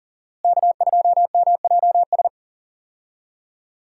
Morse code, heard as a keyed tone, about 40 wpm, 700 Hz. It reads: K2MJS
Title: USA Call Signs - 5 Characters - Repeated 3x - 40wpm